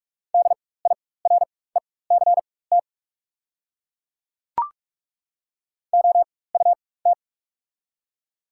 Morse code, heard as a keyed tone, about 43 wpm, spaced as for 21 wpm, 690 Hz. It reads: DIRECT OUT
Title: 4x Character Spacing - Top 500 Words - 45wpm